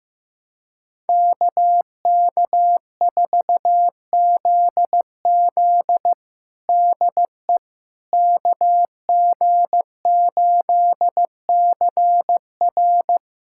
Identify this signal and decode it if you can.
Morse code, heard as a keyed tone, about 15 wpm, 700 Hz. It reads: KK4ZZ DE KG8CR
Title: USA Callsign DE Callsign - 5 Characters - 15wpm